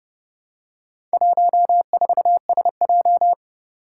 Morse code, heard as a keyed tone, about 30 wpm, 700 Hz. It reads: 14SJ
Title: Four Challenging Letter-Number Combinations 30wpm